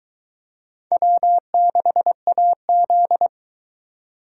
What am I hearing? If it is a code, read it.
Morse code, heard as a keyed tone, about 22 wpm, 700 Hz. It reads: W6AZ